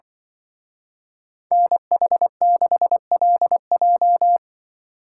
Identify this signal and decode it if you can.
Morse code, heard as a keyed tone, about 25 wpm, 700 Hz. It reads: NH6LJ
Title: Speed Racing - USA 2x2 Call Signs 20wpm